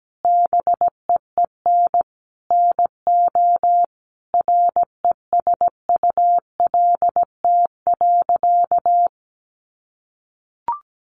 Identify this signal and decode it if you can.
Morse code, heard as a keyed tone, about 17 wpm, 700 Hz. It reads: BEEN NO RESULT.
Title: Sentences from Top 700 Words 17wpm